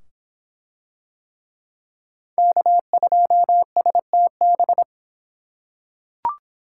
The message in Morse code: K2STB

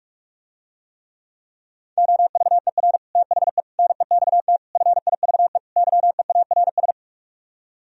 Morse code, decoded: OVER THE NEXT FIVE YEARS

